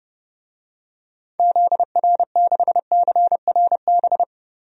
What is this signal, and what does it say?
Morse code, heard as a keyed tone, about 30 wpm, 700 Hz. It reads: ZR6CRB
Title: DXCC-100 Call Signs - 6 Characters - 30wpm